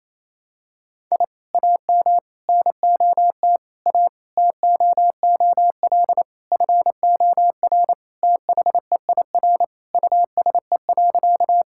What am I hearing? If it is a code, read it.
Morse code, heard as a keyed tone, about 28 wpm, 700 Hz. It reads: I AM NOT A TOOL FOR THEIR USE.
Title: Sentences from Top 1000 Words - Review Part 6 - 28wpm